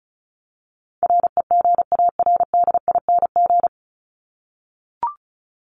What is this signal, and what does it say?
Morse code, heard as a keyed tone, about 34 wpm, 700 Hz. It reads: REGARDING
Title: ING Suffix - Top 100 Words with Base Words 35wpm